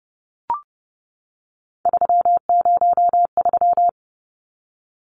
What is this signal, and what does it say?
Morse code, heard as a keyed tone, about 30 wpm, 700 Hz. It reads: 303
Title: Numbers - 3-Digit 30wpm